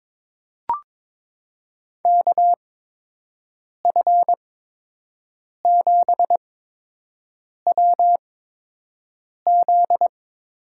Morse code, heard as a keyed tone, about 22 wpm, spaced as for 6 wpm, 700 Hz. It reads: KF7WZ
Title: USA Call Signs - 6 Characters - 8x Character Spacing - 22wpm